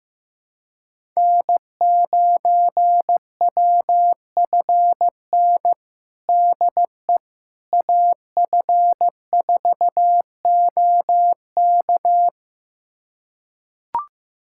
Morse code, heard as a keyed tone, about 15 wpm, 700 Hz. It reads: N9WFN DE AF4OK
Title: USA Callsign DE Callsign - 5 Characters - 15wpm